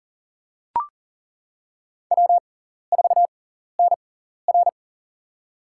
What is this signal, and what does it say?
Morse code, heard as a keyed tone, about 42 wpm, spaced as for 14 wpm, 690 Hz. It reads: W4NR
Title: USA Call Signs - 4 Characters - 6x Character Spacing - 40wpm